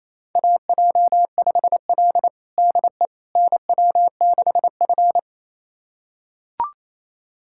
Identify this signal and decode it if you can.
Morse code, heard as a keyed tone, about 27 wpm, 700 Hz. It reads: AJ5L DE NW6F
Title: USA Callsign DE Callsign - 4 Characters - 28wpm